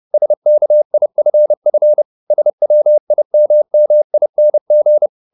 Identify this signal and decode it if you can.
Morse code, heard as a keyed tone, about 30 wpm, 600 Hz. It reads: SKIFF SWIMMING